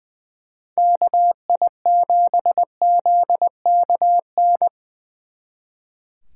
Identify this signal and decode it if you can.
Morse code, heard as a keyed tone, about 25 wpm, 700 Hz. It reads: KI7ZKN
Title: Speed Racing - USA Call Signs 20wpm